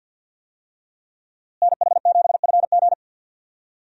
Morse code, heard as a keyed tone, about 51 wpm, 700 Hz. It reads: NH7FG
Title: USA Call Signs - 5 Characters - Repeated 2x - 50wpm